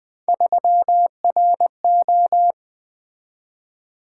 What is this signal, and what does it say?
Morse code, heard as a keyed tone, about 20 wpm, 700 Hz. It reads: 3RO
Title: Three Letter-Number Combinations - 20wpm